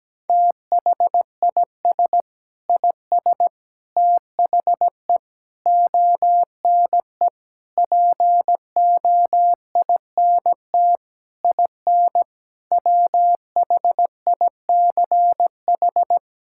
Morse code, heard as a keyed tone, about 17 wpm, 700 Hz. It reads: THIS IS THE ONE POINT IN WHICH
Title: Sentences from Top 1000 Words - Review Part 5 - 17wpm